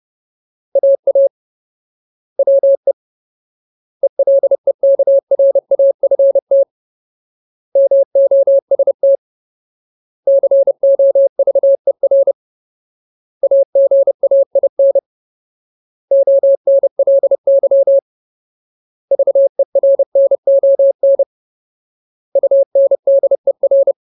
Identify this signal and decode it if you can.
Morse code, heard as a keyed tone, about 30 wpm, 550 Hz. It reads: AA WE ELEKRAFT MOST COVER AGAIN ONLY VERNON UNDER
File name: Ultimate Word List / 30WPM Ultimate